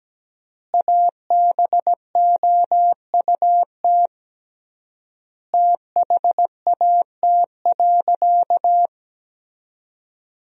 Morse code, heard as a keyed tone, about 17 wpm, 700 Hz. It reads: ABOUT THAT.
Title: 3x Word-Spacing - Sentences from Top 200 Words - 17wpm